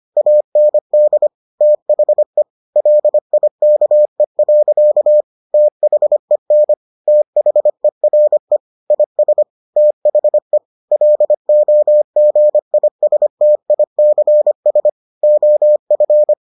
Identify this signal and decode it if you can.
Morse code, heard as a keyed tone, about 25 wpm, 600 Hz. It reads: AND THE LIKE. THEN THERE IS THE LOGISTICS OF